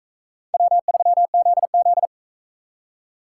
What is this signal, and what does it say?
Morse code, heard as a keyed tone, about 42 wpm, 700 Hz. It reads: W3ZZ